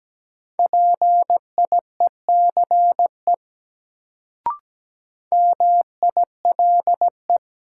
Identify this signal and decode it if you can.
Morse code, heard as a keyed tone, about 17 wpm, 700 Hz. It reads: PIECE MILE
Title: Sentences from Top 400 Words - New Word Drill - 17wpm